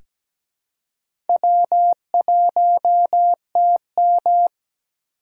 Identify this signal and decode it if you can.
Morse code, heard as a keyed tone, about 17 wpm, 700 Hz. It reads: W1TM
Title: USA Call Signs - 4 Characters - 6x Character Spacing - 17wpm